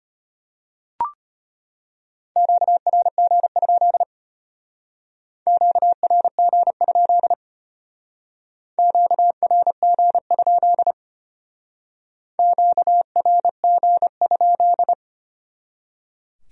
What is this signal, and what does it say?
Morse code, heard as a keyed tone, about 31 wpm, 700 Hz. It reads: QRG? QRG? QRG? QRG?